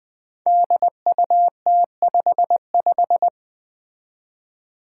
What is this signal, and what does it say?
Morse code, heard as a keyed tone, about 20 wpm, 700 Hz. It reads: DUT55
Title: Five Challenging Letter-Number Combinations 20wpm